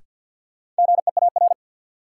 Morse code, heard as a keyed tone, about 51 wpm, 700 Hz. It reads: GEAR
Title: Sentences from Top 1400 Words - New Word Drill - 50wpm